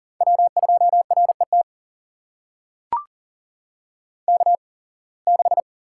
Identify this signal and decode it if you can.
Morse code, heard as a keyed tone, about 40 wpm, spaced as for 26 wpm, 700 Hz. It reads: W2RET K6
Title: USA Call Signs - 8x Character Spacing - 40wpm